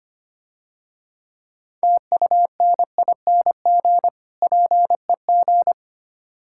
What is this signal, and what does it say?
Morse code, heard as a keyed tone, about 27 wpm, 700 Hz. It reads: TUNING PEG
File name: practice / 25wpm / Difficult Words - Speed-Racing with Hints - 25wpm